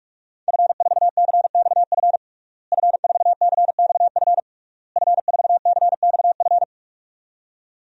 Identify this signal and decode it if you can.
Morse code, heard as a keyed tone, about 42 wpm, 700 Hz. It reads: F4CXF F4CXF F4CXF
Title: DXCC-100 Call Signs - 5 Characters - Repeated 3x - 45wpm